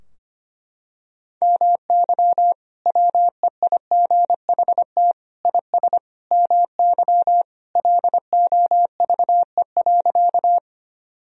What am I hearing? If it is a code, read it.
Morse code, heard as a keyed tone, about 25 wpm, 700 Hz. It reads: MY WEIGHT IS MY LOVE.
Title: Sentences from Top 1000 Words - Review Part 3 - 25wpm